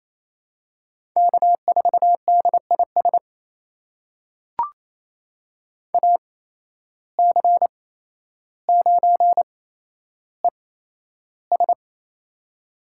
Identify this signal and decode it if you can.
Morse code, heard as a keyed tone, about 27 wpm, spaced as for 12 wpm, 700 Hz. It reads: K4DIS AC9ES
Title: USA Call Signs - 8x Character Spacing - 28wpm